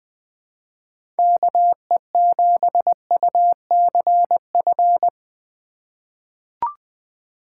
Morse code, heard as a keyed tone, about 20 wpm, 700 Hz. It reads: KE7UCF
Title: USA Call Signs - 6 Characters - 3x Character Spacing - 20wpm